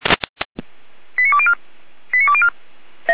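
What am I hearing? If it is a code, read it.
Five-tone selective call (ZVEI), heard as quick runs of five tones, tones of about 70 ms. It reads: 89204 89204
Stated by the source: ZVEI five-tone call 89169, received by SDR.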